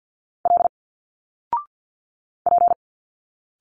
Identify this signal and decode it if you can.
Morse code, heard as a keyed tone, about 50 wpm, 700 Hz. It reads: L P
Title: Single Letters 50wpm